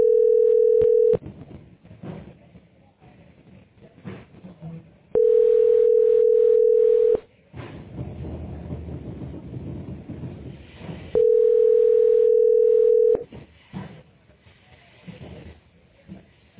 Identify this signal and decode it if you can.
Ringback tone (North America), 440 + 480 Hz.